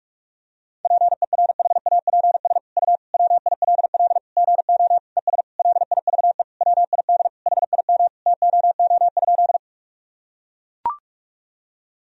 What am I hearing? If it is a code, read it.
Morse code, heard as a keyed tone, about 42 wpm, 700 Hz. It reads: PERHAPS U WILL GO ES LIVE WID HIM TOO?
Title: Sentences from Top 500 Words and Abbreviations 45wpm